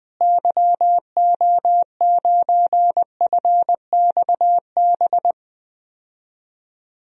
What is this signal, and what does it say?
Morse code, heard as a keyed tone, about 20 wpm, 700 Hz. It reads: YO9FXB
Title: DXCC-100 Call Signs - 6 Characters - Repeated 3x - 20wpm